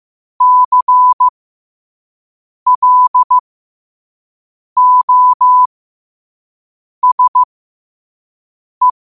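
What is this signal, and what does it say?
Morse code, heard as a keyed tone, about 15 wpm, spaced as for 5 wpm, 1000 Hz. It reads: CLOSE
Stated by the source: 5 wpm Morse.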